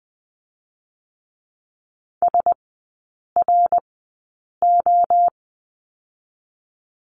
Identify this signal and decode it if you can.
Morse code, heard as a keyed tone, about 20 wpm, spaced as for 8 wpm, 700 Hz. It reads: SRO
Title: 060 Lesson 11e - Letter h - Three Characters 20wpm